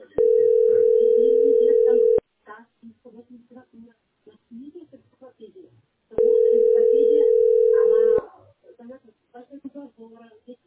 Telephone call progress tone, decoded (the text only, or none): ringback tone